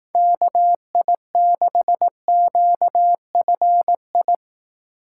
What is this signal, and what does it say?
Morse code, heard as a keyed tone, about 19 wpm, 700 Hz. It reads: KI6QFI